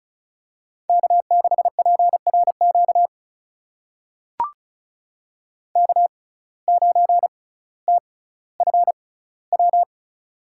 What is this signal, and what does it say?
Morse code, heard as a keyed tone, about 36 wpm, spaced as for 18 wpm, 700 Hz. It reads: K6PRQ K9TFW
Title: USA Call Signs - 5 Characters - 6x Character Spacing - 35wpm